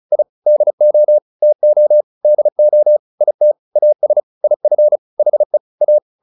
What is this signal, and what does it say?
Morse code, heard as a keyed tone, about 34 wpm, 600 Hz. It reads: I DO TO DO IT AS IF HE A
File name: WD201 35